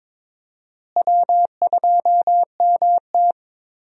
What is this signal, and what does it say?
Morse code, heard as a keyed tone, about 21 wpm, 700 Hz. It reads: W2MT
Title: USA Call Signs - 4 Characters - Repeated 2x - 22wpm